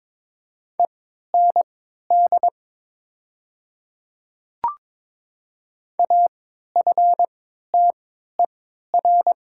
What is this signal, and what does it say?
Morse code, heard as a keyed tone, about 21 wpm, spaced as for 12 wpm, 700 Hz. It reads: END AFTER
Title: 3x Character Spacing - Top 300 Words - 22wpm